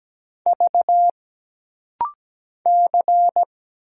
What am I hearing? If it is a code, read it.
Morse code, heard as a keyed tone, about 17 wpm, 700 Hz. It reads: V C
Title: Single Letter-Number Alt - Rapid-Fire 17wpm